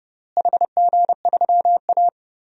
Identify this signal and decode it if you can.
Morse code, heard as a keyed tone, about 30 wpm, 700 Hz. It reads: HG3A
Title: DXCC-100 Call Signs - 4 Characters - Repeated 3x - 30wpm